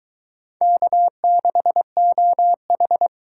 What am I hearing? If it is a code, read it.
Morse code, heard as a keyed tone, about 23 wpm, 700 Hz. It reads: K6OH